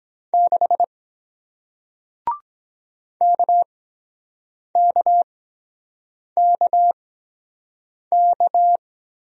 Morse code, heard as a keyed tone, about 23 wpm, spaced as for 7 wpm, 700 Hz. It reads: 6 KKKK